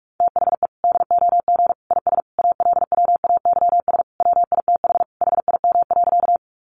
Morse code, heard as a keyed tone, about 44 wpm, 700 Hz. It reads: THE DOG IS ALWAYS WITH HIM.
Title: Sentences from Top 400 Words 45wpm